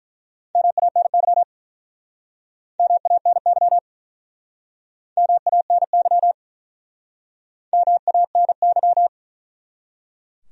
Morse code, heard as a keyed tone, about 48 wpm, 700 Hz. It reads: MANY MANY MANY MANY